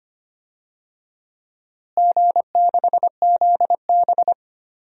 Morse code, heard as a keyed tone, about 25 wpm, 700 Hz. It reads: G6ZB
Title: DXCC-100 Call Signs - 4 Characters - Repeated 3x - 25wpm